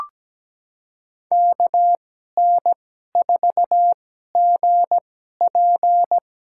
Morse code, heard as a keyed tone, about 17 wpm, spaced as for 12 wpm, 700 Hz. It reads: KN4GP
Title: USA Call Signs - 5 Characters - 2x Character Spacing - 17wpm